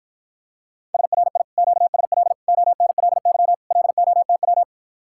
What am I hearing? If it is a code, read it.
Morse code, heard as a keyed tone, about 54 wpm, 700 Hz. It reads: SRI QSL ONLY LOTW